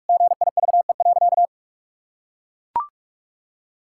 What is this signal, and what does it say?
Morse code, heard as a keyed tone, about 41 wpm, 700 Hz. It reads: GIVE.
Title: Sentences from Top 1000 Words - Review Part 6 - 45wpm